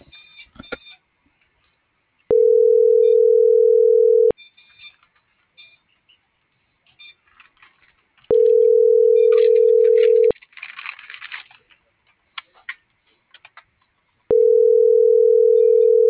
Ringback tone (North America), 440 + 480 Hz.